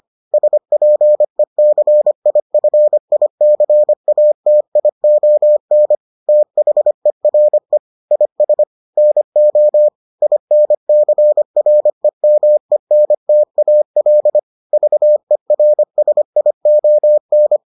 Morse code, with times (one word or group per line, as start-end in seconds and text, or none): 0.3-5.9 SPECIFICATION
6.3-7.8 THERE
8.1-8.6 IS
9.0-9.9 NO
10.2-14.4 INCREMENTAL
14.7-17.6 VERSION